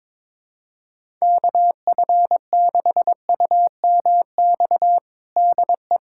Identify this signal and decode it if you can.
Morse code, heard as a keyed tone, about 22 wpm, 700 Hz. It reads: KF6UMX DE
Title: USA Callsign DE Callsign - 6 Characters - 22wpm